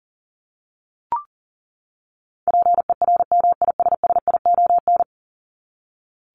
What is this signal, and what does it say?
Morse code, heard as a keyed tone, about 40 wpm, 700 Hz. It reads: PERMISSION
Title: Top 1501-1750 Words 40wpm